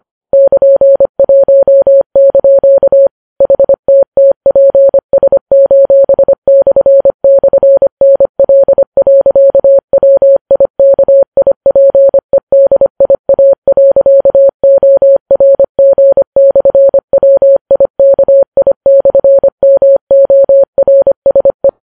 Morse code, heard as a keyed tone, about 25 wpm, 550 Hz. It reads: <KN>1) HTTPS://NL.WIKIPEDIA.ORG/WIKI/MORSE